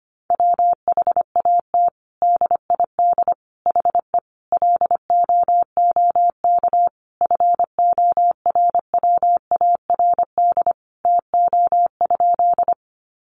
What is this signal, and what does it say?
Morse code, heard as a keyed tone, about 25 wpm, 700 Hz. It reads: WHAT DID HE LOOK FORWARD TO?
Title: Sentences from Top 1300 Words 25wpm